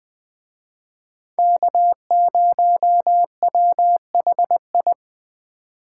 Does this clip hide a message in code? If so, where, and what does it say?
Yes: Morse code from 1.4 to 4.9 s, heard as a keyed tone, about 20 wpm, 700 Hz. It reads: K0WHI